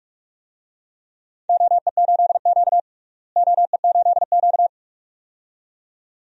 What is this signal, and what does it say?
Morse code, heard as a keyed tone, about 43 wpm, 700 Hz. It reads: OE8Q OE8Q
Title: DXCC-100 Call Signs - 4 Characters - Repeated 2x - 45wpm